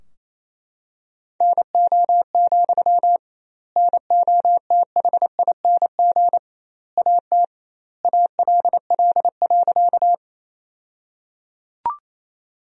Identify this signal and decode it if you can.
Morse code, heard as a keyed tone, about 28 wpm, 700 Hz. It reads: NO, NOTHING AT ALL.
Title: 2x Word-Spacing - Sentences from Top 500 Words - 28wpm